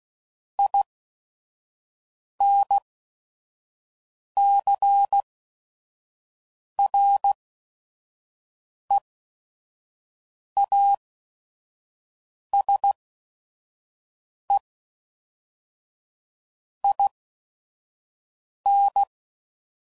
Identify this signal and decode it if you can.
Morse code, heard as a keyed tone, about 16 wpm, spaced as for 5 wpm, 800 Hz. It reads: INCREASE IN
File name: CW Code Groups @5WPM / Technical Subjects / XC